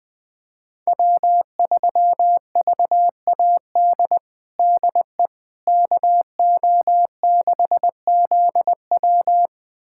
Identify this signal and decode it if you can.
Morse code, heard as a keyed tone, about 20 wpm, 700 Hz. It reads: W3VAD DE KO6ZW
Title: USA Callsign DE Callsign - 5 Characters - 20wpm